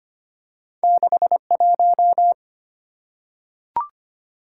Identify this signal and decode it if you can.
Morse code, heard as a keyed tone, about 25 wpm, 700 Hz. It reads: 61
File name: Two Letter-Number Combinations - 25wpm